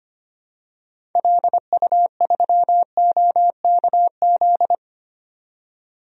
Morse code, heard as a keyed tone, about 25 wpm, 700 Hz. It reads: LU3OKZ
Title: DXCC-100 Call Signs - 6 Characters - 25wpm